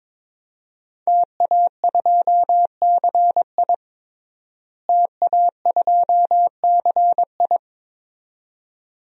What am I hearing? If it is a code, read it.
Morse code, heard as a keyed tone, about 21 wpm, 700 Hz. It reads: TA2CI TA2CI
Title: DXCC-100 Call Signs - 5 Characters - Repeated 2x - 22wpm